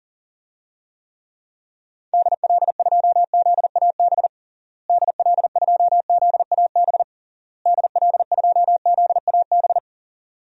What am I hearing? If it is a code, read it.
Morse code, heard as a keyed tone, about 40 wpm, 700 Hz. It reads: DL2ZAB DL2ZAB DL2ZAB
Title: DXCC-100 Call Signs - 6 Characters - Repeated 3x - 40wpm